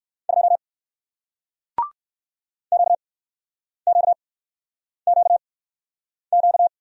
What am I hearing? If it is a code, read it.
Morse code, heard as a keyed tone, about 60 wpm, spaced as for 9 wpm, 700 Hz. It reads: 2 QQQQ